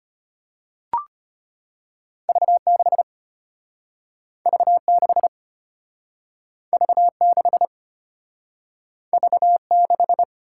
Morse code, heard as a keyed tone, about 34 wpm, 700 Hz. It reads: V6 V6 V6 V6